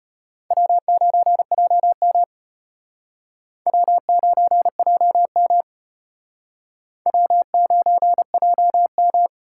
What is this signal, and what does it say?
Morse code, heard as a keyed tone, about 31 wpm, 700 Hz. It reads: W9JM W9JM W9JM